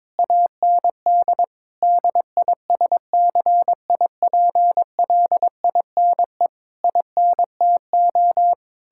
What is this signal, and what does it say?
Morse code, heard as a keyed tone, about 22 wpm, 700 Hz. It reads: AND DISCIPLINE INTO
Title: Sentences from Top 1000 Words - Review Part 6 - 22wpm